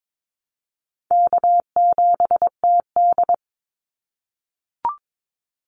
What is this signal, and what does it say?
Morse code, heard as a keyed tone, about 22 wpm, 700 Hz. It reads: K7TD
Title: Call Signs - 22wpm